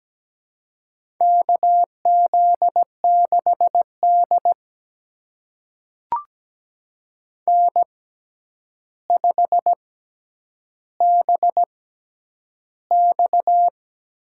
Morse code, heard as a keyed tone, about 17 wpm, spaced as for 9 wpm, 700 Hz. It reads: KZ6D N5BX
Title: USA Call Signs - 4 Characters - 6x Character Spacing - 17wpm